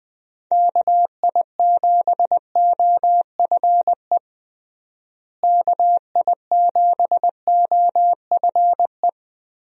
Morse code, heard as a keyed tone, about 21 wpm, 700 Hz. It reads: KI7OFE KI7OFE
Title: USA Call Signs - 6 Characters - Repeated 3x - 20wpm